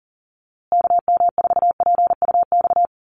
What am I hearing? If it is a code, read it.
Morse code, heard as a keyed tone, about 40 wpm, 700 Hz. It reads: KM4PUX
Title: USA Call Signs Repeated 3x 40wpm